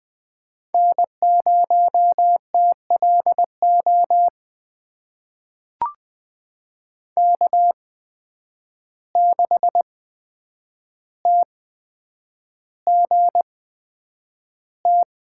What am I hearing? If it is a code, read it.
Morse code, heard as a keyed tone, about 20 wpm, spaced as for 9 wpm, 700 Hz. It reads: N0TLO K6TGT
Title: USA Call Signs - 5 Characters - 8x Character Spacing - 20wpm